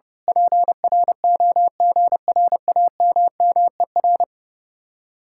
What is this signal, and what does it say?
Morse code, heard as a keyed tone, about 29 wpm, 700 Hz. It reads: PROGRAMMER
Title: Speed-Racing Common QSO Phrases Encore - 30wpm